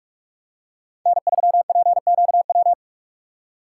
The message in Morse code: N3PQW

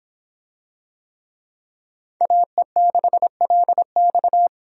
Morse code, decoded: AE6LX

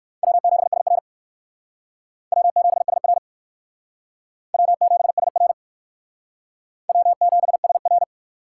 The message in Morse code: W7SR W7SR W7SR W7SR

